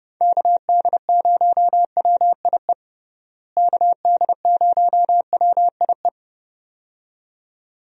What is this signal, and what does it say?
Morse code, heard as a keyed tone, about 30 wpm, 700 Hz. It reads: KD0WIE KD0WIE
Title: USA Call Signs - 6 Characters - Repeated 2x - 30wpm